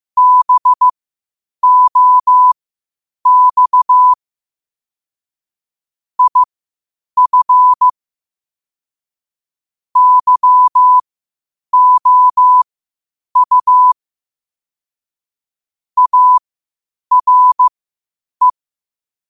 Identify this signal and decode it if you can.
Morse code, heard as a keyed tone, about 15 wpm, spaced as for 8 wpm, 1000 Hz. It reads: BOX IF YOU ARE
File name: qotdfiles / 8wpm